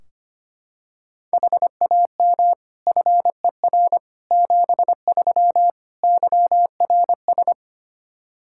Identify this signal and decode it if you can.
Morse code, heard as a keyed tone, about 25 wpm, 700 Hz. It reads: HAM FER 73 YRS
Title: Practice QSO - Exchange 3 - More Info - 25wpm